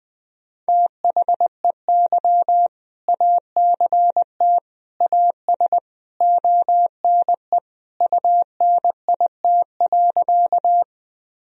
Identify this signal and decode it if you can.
Morse code, heard as a keyed tone, about 21 wpm, 700 Hz. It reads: THEY ACT AS ONE UNIT.
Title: Sentences from Top 1000 Words - Review Part 2 - 20wpm